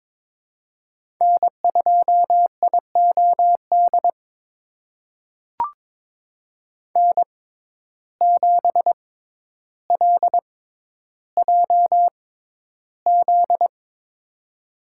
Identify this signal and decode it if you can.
Morse code, heard as a keyed tone, about 22 wpm, spaced as for 11 wpm, 700 Hz. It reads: N2IOD N7LJZ
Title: USA Call Signs - 5 Characters - 6x Character Spacing - 22wpm